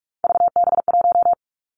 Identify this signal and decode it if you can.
Morse code, heard as a keyed tone, about 46 wpm, 700 Hz. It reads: VB1
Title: Three Challenging Letter-Number Combinations 45wpm